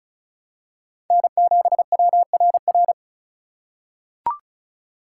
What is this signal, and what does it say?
Morse code, heard as a keyed tone, about 34 wpm, 700 Hz. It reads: N7WRR
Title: USA Call Signs - 5 Characters - 2x Character Spacing - 35wpm